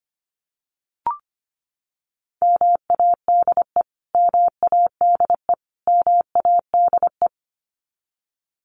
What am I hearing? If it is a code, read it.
Morse code, heard as a keyed tone, about 25 wpm, 700 Hz. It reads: MADE MADE MADE
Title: Top 500 Words Repeated 25wpm